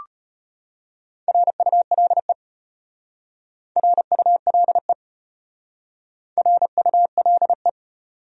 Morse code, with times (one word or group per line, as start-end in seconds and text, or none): 1.3-2.3 RULE
3.8-4.9 RULE
6.4-7.7 RULE